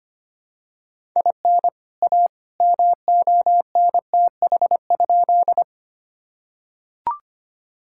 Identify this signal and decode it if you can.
Morse code, heard as a keyed tone, about 25 wpm, 700 Hz. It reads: IN A MONTH?
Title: Sentences from Top 1000 Words - Review Part 5 - 25wpm